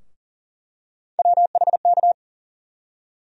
Morse code, heard as a keyed tone, about 38 wpm, 700 Hz. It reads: WHK
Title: USA Call Sign Postfixes - 3 Letters 40wpm